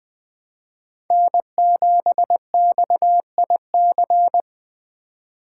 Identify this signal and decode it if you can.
Morse code, heard as a keyed tone, about 20 wpm, 700 Hz. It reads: N7XIC
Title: Speed Racing - USA 1x3 Call Signs 17wpm